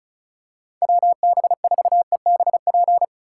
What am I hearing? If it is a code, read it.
Morse code, heard as a keyed tone, about 34 wpm, 700 Hz. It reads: WB4EBP
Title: USA Call Signs - 6 Characters - 2x Character Spacing - 35wpm